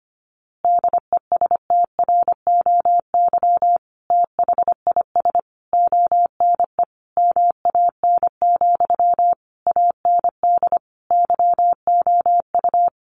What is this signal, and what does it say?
Morse code, heard as a keyed tone, about 25 wpm, 700 Hz. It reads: DESTROY THIS ONE MAN, AND YOU
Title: A Sound Of Thunder 25wpm